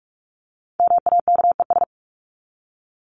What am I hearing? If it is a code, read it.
Morse code, heard as a keyed tone, about 44 wpm, 700 Hz. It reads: MAKES
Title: ES:S Suffix - Top 100 Words with Base Words 45wpm